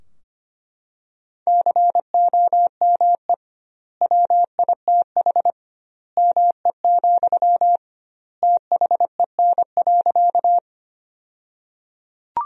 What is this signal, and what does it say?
Morse code, heard as a keyed tone, about 25 wpm, 700 Hz. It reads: COME WITH ME, THEN.
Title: 2x Word-Spacing - Sentences from Top 200 Words - 25wpm